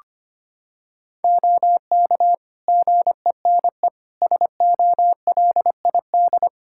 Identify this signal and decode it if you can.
Morse code, heard as a keyed tone, about 25 wpm, 700 Hz. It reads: OK GENE SOLID
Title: Practice QSO - Exchange 3 - More Info - 25wpm